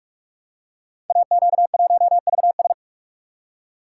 Morse code, heard as a keyed tone, about 44 wpm, 700 Hz. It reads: AQ1VS
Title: Code Groups - 5 Letter:Numbers - 45wpm